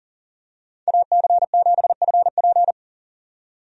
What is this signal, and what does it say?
Morse code, heard as a keyed tone, about 40 wpm, 700 Hz. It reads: AC7FP
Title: USA Call Signs - 5 Characters - 40wpm